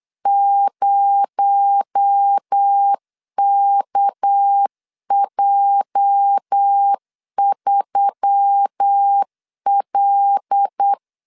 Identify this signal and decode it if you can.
Morse code, heard as a keyed tone, about 8 wpm, 800 Hz. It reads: 0KJ3L
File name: LEZIONI / 10 SIGNS / 10B INT 8wpm 3534